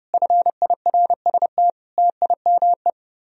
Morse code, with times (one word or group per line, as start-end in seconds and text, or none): 0.1-1.7 FIRST
2.0-2.9 TIME